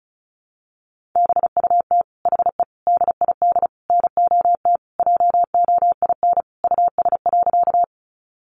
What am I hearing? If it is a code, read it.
Morse code, heard as a keyed tone, about 34 wpm, 700 Hz. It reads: BUT HE DID NOT JOIN US.
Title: Sentences from Top 1400 Words 35wpm